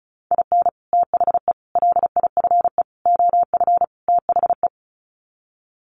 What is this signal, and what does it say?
Morse code, heard as a keyed tone, about 34 wpm, 700 Hz. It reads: IN THE LIFE OF THE
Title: Sets of 5 Words Encore2 35wpm